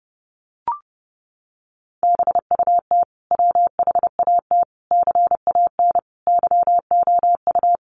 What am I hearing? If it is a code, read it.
Morse code, heard as a keyed tone, about 30 wpm, 700 Hz. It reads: BUT WHAT CAN YOU
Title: Sentences from Top 1100 Words 30wpm